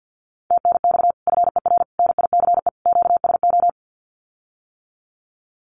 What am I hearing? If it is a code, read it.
Morse code, heard as a keyed tone, about 50 wpm, 700 Hz. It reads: TNX FER NICE QSO
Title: Common QSO Phrases Encore 50wpm